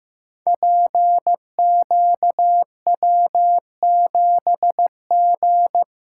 Morse code, heard as a keyed tone, about 15 wpm, 700 Hz. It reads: PQW7G
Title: Code Groups - 5 Letter:Numbers - 15wpm